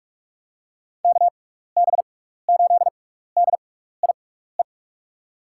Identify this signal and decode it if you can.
Morse code, heard as a keyed tone, about 45 wpm, spaced as for 15 wpm, 700 Hz. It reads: KB8DIE
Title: USA Call Signs - 6x Character Spacing - 45wpm